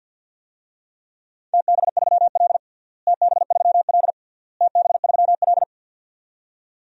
Morse code, heard as a keyed tone, about 53 wpm, 700 Hz. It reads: TB3L TB3L TB3L
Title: DXCC-100 Call Signs - 4 Characters - Repeated 3x - 50wpm